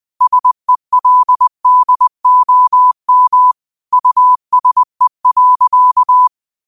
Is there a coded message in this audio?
Yes: Morse code from 0.2 to 6.3 s, heard as a keyed tone, about 20 wpm, 1000 Hz. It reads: SELDOM USE.